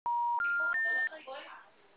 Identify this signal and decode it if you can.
Special information tone, with speech.
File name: out- -2009- - - .78306